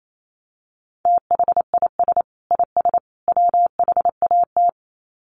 Morse code, heard as a keyed tone, about 28 wpm, 700 Hz. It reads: THIS IS WHAT
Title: Sets of 3 Words 28wpm